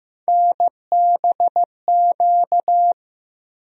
Morse code, heard as a keyed tone, about 15 wpm, 700 Hz. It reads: NBQ
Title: USA Call Sign Postfixes - 3 Letters 15wpm